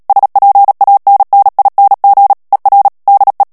Morse code, heard as a keyed tone, about 37 wpm, 800 Hz. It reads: SPANNING ER DE
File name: morse mp3 / HOLLANDAIS texte clair 2 vitesse 35 mots minute